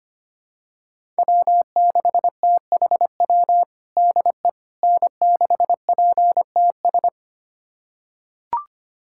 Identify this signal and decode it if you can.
Morse code, heard as a keyed tone, about 25 wpm, 700 Hz. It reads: W6THW DE N6PTS
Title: USA Callsign DE Callsign - 5 Characters - 25wpm